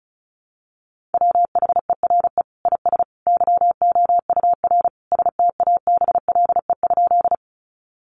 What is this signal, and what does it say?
Morse code, heard as a keyed tone, about 34 wpm, 700 Hz. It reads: WHERE IS YOUR STABLE?
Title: Sentences from Top 1500 Words 35wpm